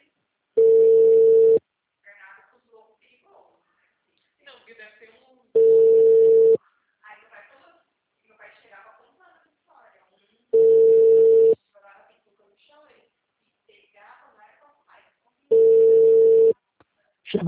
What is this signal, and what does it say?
Ringback tone (China, 450 Hz), with speech.